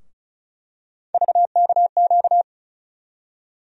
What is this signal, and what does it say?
Morse code, heard as a keyed tone, about 36 wpm, 700 Hz. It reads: VKQ
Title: USA Call Sign Postfixes - 3 Letters 35wpm